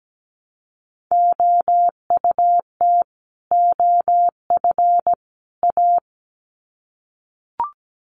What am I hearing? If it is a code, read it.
Morse code, heard as a keyed tone, about 17 wpm, 700 Hz. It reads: OUT OF A
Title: Sets of 3 Words 17wpm